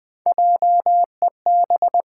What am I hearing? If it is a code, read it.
Morse code, heard as a keyed tone, about 21 wpm, 700 Hz. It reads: JEB
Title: Pangram Challenge v2 - 20wpm